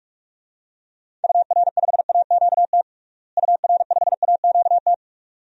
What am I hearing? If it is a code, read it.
Morse code, heard as a keyed tone, about 41 wpm, 700 Hz. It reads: UR5AQT UR5AQT
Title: DXCC-100 Call Signs - 6 Characters - Repeated 2x - 45wpm